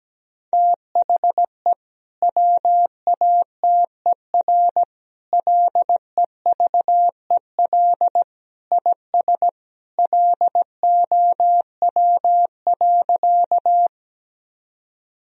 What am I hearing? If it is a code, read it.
Morse code, heard as a keyed tone, about 17 wpm, 700 Hz. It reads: THE WATER LEVEL IS LOW.
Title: Sentences from Top 1000 Words - Review Part 4 - 17wpm